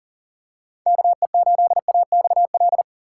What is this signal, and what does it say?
Morse code, heard as a keyed tone, about 40 wpm, 700 Hz. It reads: KE8AXL
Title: USA Call Signs - 6 Characters - 3x Character Spacing - 40wpm